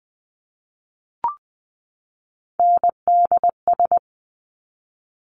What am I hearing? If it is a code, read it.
Morse code, heard as a keyed tone, about 20 wpm, 700 Hz. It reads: NDS
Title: Top Three Letter Combinations in Words 20wpm